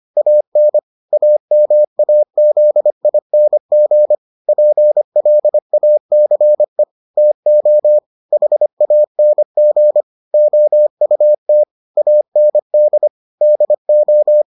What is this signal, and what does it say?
Morse code, heard as a keyed tone, about 25 wpm, 600 Hz. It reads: AN AMAZING PLACE TO HANG OUT AND DO